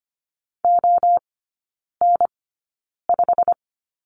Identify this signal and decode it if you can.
Morse code, heard as a keyed tone, about 25 wpm, spaced as for 9 wpm, 690 Hz. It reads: ON5
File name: Lesson 15e - Number 5 - Three Characters 25wpm